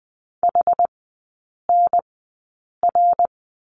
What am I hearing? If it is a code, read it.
Morse code, heard as a keyed tone, about 20 wpm, spaced as for 8 wpm, 700 Hz. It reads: HNR
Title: 087 Lesson 15e - Number 5 - Three Characters 20wpm